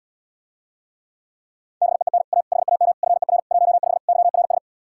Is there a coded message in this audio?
Yes: Morse code from 1.8 to 4.6 s, heard as a keyed tone, about 79 wpm, 700 Hz. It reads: BEEN A HAM FER 15 YRS